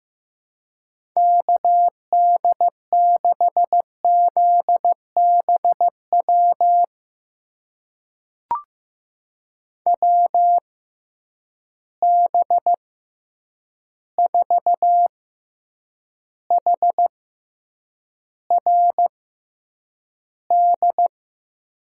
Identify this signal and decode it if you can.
Morse code, heard as a keyed tone, about 15 wpm, spaced as for 8 wpm, 700 Hz. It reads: KD6ZBW WB4HRD
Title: USA Call Signs - 6 Characters - 6x Character Spacing - 15wpm